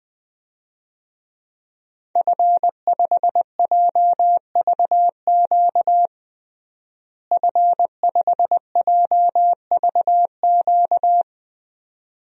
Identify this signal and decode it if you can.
Morse code, heard as a keyed tone, about 20 wpm, 700 Hz. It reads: F5JVQ F5JVQ
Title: DXCC-100 Call Signs - 5 Characters - Repeated 2x - 20wpm